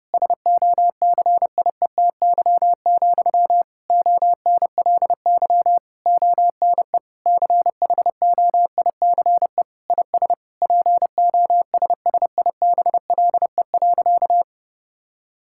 Morse code, heard as a keyed tone, about 30 wpm, 700 Hz. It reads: SOCIETY, ONLY ONE CHOICE IS POSSIBLE.
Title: Sentences from Top 1000 Words - Review Part 6 - 30wpm